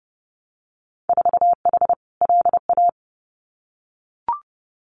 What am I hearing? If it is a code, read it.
Morse code, heard as a keyed tone, about 30 wpm, 700 Hz. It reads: 4H LA
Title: Winter Field Day Exchanges Encore 30wpm